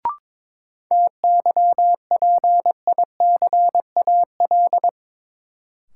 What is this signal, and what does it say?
Morse code, heard as a keyed tone, about 22 wpm, 700 Hz. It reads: TYPICAL
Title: Sentences from Top 1100 Words - New Word Drill Alt - 22wpm